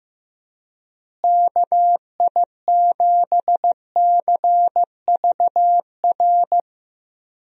Morse code, heard as a keyed tone, about 15 wpm, 700 Hz. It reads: KI7CVR
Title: USA Call Signs - 6 Characters - Repeated 2x - 15wpm